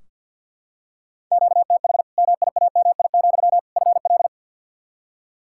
Morse code, heard as a keyed tone, about 46 wpm, 700 Hz. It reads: QTH MIAMI, FL
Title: Practice QSO - Exchange 1 - Beginning - 50wpm